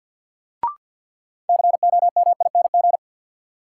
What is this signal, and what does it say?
Morse code, heard as a keyed tone, about 52 wpm, 700 Hz. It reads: COMING
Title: Sentences from Top 1300 Words - New Word Drill Alt - 50wpm